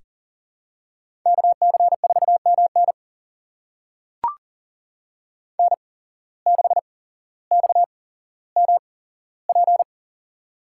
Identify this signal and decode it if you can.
Morse code, heard as a keyed tone, about 40 wpm, spaced as for 17 wpm, 700 Hz. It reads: KC4MN N6XMP
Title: USA Call Signs - 8x Character Spacing - 40wpm